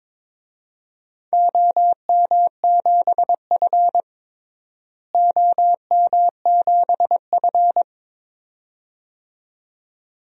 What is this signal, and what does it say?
Morse code, heard as a keyed tone, about 21 wpm, 700 Hz. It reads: OM7F OM7F
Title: DXCC-100 Call Signs - 4 Characters - Repeated 2x - 22wpm